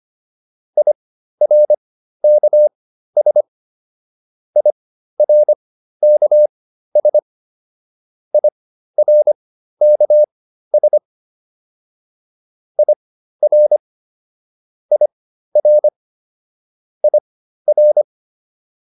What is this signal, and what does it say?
Morse code, heard as a keyed tone, about 25 wpm, spaced as for 13 wpm, 600 Hz. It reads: IRKS IRKS IRKS IR IR IR